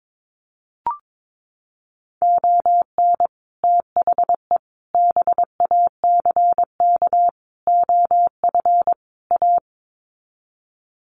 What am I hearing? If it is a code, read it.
Morse code, heard as a keyed tone, about 22 wpm, 700 Hz. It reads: ON THE BACK OF A
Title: Sets of 5 Words Encore2 22wpm